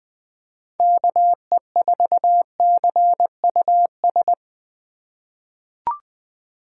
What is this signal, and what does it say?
Morse code, heard as a keyed tone, about 20 wpm, 700 Hz. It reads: KE4CUS
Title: USA Call Signs - 6 Characters - 4x Character Spacing - 20wpm